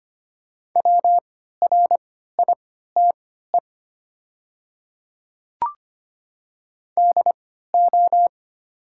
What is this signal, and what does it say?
Morse code, heard as a keyed tone, about 25 wpm, spaced as for 14 wpm, 700 Hz. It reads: WRITE DO
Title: 3x Character Spacing - Top 100 Words - 25wpm